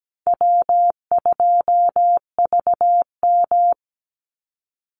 Morse code, heard as a keyed tone, about 17 wpm, 700 Hz. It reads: W2VM
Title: Easy Call Signs Repeated 3x - 17wpm